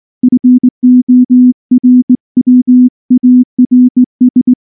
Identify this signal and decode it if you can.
Morse code, heard as a keyed tone, about 19 wpm, 250 Hz. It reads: FORWARS
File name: morse / morse-training-20WPM-6